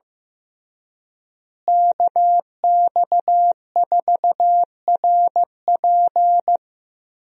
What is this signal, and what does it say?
Morse code, heard as a keyed tone, about 15 wpm, 700 Hz. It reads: KX4RP
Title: USA Call Signs - 5 Characters - Repeated 2x - 15wpm